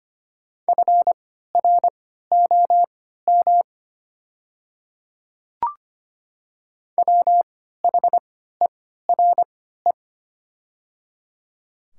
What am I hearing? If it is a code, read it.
Morse code, heard as a keyed tone, about 25 wpm, spaced as for 14 wpm, 700 Hz. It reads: FROM WHERE
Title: 3x Character Spacing - Top 500 Words - 25wpm